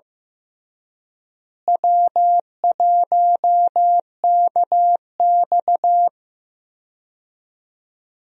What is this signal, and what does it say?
Morse code, heard as a keyed tone, about 15 wpm, 700 Hz. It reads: W1KX
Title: USA Call Signs - 4 Characters - Repeated 2x - 15wpm